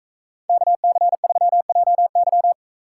Morse code, heard as a keyed tone, about 44 wpm, 700 Hz. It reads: KC3JY